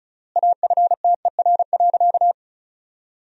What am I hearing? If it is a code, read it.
Morse code, heard as a keyed tone, about 36 wpm, 700 Hz. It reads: AFTER.
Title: 3x Word-Spacing - Sentences from Top 500 Words - 35wpm